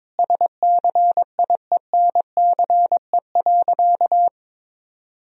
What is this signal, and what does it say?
Morse code, heard as a keyed tone, about 22 wpm, 700 Hz. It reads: SCIENCE.
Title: Sentences from Top 1000 Words - Review Part 2 - 22wpm